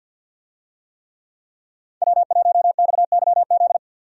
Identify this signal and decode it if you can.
Morse code, heard as a keyed tone, about 53 wpm, 700 Hz. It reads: W1XYZ
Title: USA Call Signs - 5 Characters - 50wpm